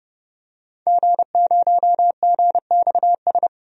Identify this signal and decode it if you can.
Morse code, heard as a keyed tone, about 29 wpm, 700 Hz. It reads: G0GXS
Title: DXCC-100 Call Signs - 5 Characters - 30wpm